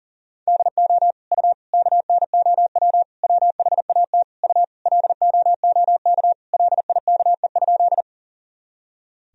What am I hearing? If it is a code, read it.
Morse code, heard as a keyed tone, about 37 wpm, 700 Hz. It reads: DO U KNOW WHAT U LOOK LIKE?
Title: Sentences from Top 100 Words and Abbreviations 40wpm